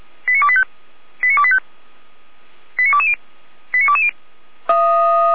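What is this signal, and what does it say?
Five-tone selective call (ZVEI), heard as quick runs of five tones, tones of about 70 ms. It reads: 89285 89285 89220 89220
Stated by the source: Five-tone ZVEI selective call 89501 received by SDR.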